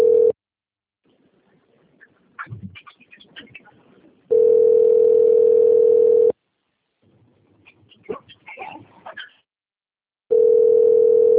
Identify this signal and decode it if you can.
Ringback tone (North America), 440 + 480 Hz.